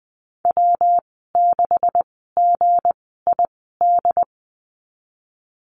Morse code, heard as a keyed tone, about 20 wpm, spaced as for 15 wpm, 700 Hz. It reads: W6GID
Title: 173 Lesson 28g - Number 9 - Call Signs 20wpm